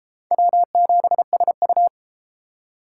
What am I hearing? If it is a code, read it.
Morse code, heard as a keyed tone, about 30 wpm, 700 Hz. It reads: W7SU